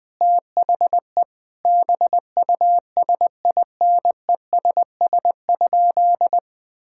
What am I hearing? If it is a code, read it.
Morse code, heard as a keyed tone, about 20 wpm, 700 Hz. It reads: THE BUSINESS?
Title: Sentences from Top 1600 Words 20wpm